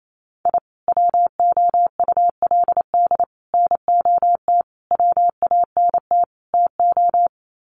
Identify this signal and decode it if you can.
Morse code, heard as a keyed tone, about 28 wpm, 700 Hz. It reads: I WOULD NOT WANT TO
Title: Sets of 5 Words 28wpm